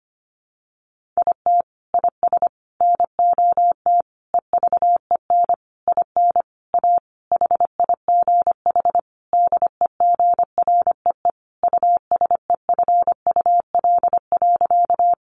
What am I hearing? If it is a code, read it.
Morse code, heard as a keyed tone, about 25 wpm, 700 Hz. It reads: IT IS NOT EVEN IN A HIGH DEGREE USEFUL.
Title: Sentences from Top 1500 Words 25wpm